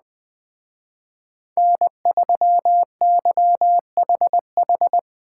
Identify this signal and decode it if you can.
Morse code, heard as a keyed tone, about 21 wpm, 700 Hz. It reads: N3YHH
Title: Speed Racing - USA 1x3 Call Signs 17wpm